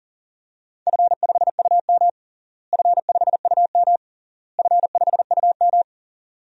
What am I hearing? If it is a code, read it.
Morse code, heard as a keyed tone, about 40 wpm, 700 Hz. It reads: F5UM F5UM F5UM
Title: DXCC-100 Call Signs - 4 Characters - Repeated 3x - 40wpm